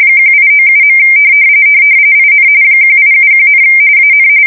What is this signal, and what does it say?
RTTY, about 45.45 baud. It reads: 1897. IN 1871, AN
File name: fsk170 / usnavy3